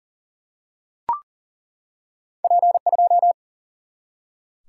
Morse code, heard as a keyed tone, about 38 wpm, 700 Hz. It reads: P 2
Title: Single-Letter Single-Number 40wpm